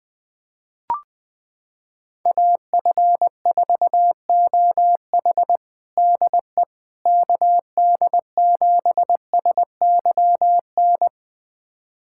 Morse code, heard as a keyed tone, about 20 wpm, 700 Hz. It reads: AF4OH DE KD7SYN
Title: USA Callsign DE Callsign 20wpm